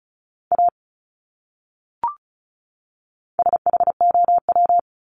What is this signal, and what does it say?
Morse code, heard as a keyed tone, about 34 wpm, 700 Hz. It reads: A SHOW
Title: Top 300 Words 35wpm